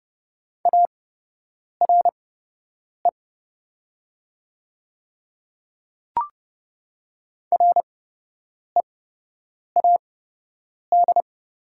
Morse code, heard as a keyed tone, about 30 wpm, spaced as for 8 wpm, 700 Hz. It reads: ARE READ
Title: 8x Character Spacing - Top 200 Words - 30wpm